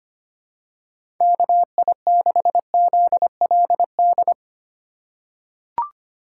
Morse code, heard as a keyed tone, about 25 wpm, 700 Hz. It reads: KI6ZLD
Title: USA Call Signs - 3x Character Spacing - 25wpm